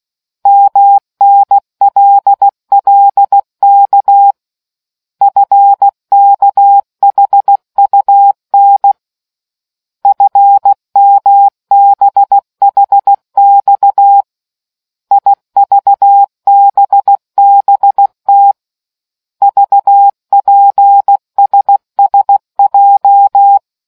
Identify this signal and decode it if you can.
Morse code, heard as a keyed tone, about 16 wpm, 800 Hz. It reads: MNLLK FKHUN FMBHX IVBBT VPSSJ